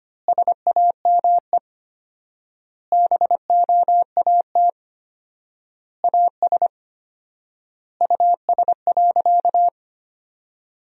Morse code, heard as a keyed tone, about 25 wpm, 700 Hz. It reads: SAME BOAT AS US.
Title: 4x Word-Spacing - Sentences from Top 500 Words - 25wpm